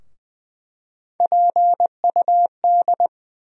Morse code, heard as a keyed tone, about 21 wpm, 700 Hz. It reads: PUD
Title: USA Call Sign Postfixes - 3 Letters 20wpm